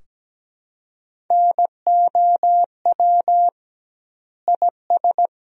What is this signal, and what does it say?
Morse code, heard as a keyed tone, about 17 wpm, 700 Hz. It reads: NOW IS
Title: 2x Word-Spacing - Sentences from Top 100 Words - 17wpm